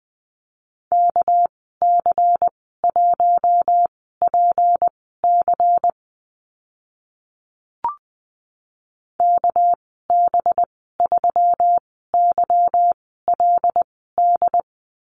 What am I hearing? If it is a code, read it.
Morse code, heard as a keyed tone, about 20 wpm, spaced as for 14 wpm, 700 Hz. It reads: KC1PC KB3YLD
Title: 200 Lesson 32g - Letter k - Call Signs 20wpm